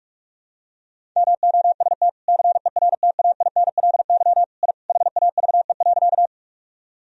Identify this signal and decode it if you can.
Morse code, heard as a keyed tone, about 46 wpm, 700 Hz. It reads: MOST CERTAINLY I HAVE.
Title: Sentences from Top 1000 Words - Review Part 1 - 45wpm